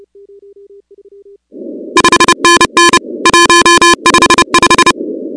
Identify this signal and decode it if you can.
Morse code, heard as a keyed tone, about 30 wpm, 1030 Hz, with noise. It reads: 5NN 155
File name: contests / 2019-cw-wpx-kt5j-k5pi- -2330 0-16